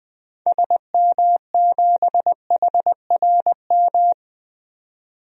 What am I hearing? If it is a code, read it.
Morse code, heard as a keyed tone, about 20 wpm, 700 Hz. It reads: SM7HRM
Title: DXCC-100 Call Signs - 6 Characters - 20wpm